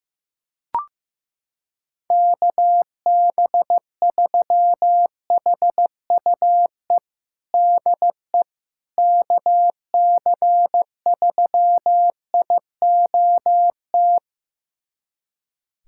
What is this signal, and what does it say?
Morse code, heard as a keyed tone, about 15 wpm, 700 Hz. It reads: KB3HUE DE KC3IOT
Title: USA Callsign DE Callsign - 6 Characters - 15wpm